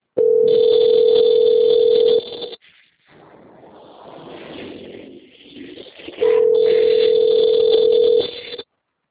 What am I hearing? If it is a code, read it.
Ringback tone (North America), 440 + 480 Hz.